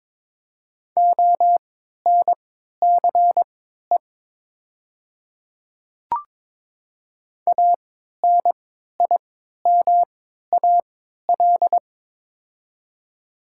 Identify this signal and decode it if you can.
Morse code, heard as a keyed tone, about 23 wpm, spaced as for 13 wpm, 700 Hz. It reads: ONCE ANIMAL
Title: 3x Character Spacing - Top 300 Words - 22wpm